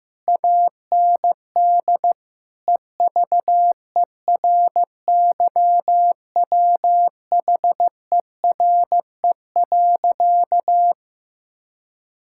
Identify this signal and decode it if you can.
Morse code, heard as a keyed tone, about 15 wpm, 700 Hz. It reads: AND EVERYWHERE.
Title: Sentences from Top 1600 Words 15wpm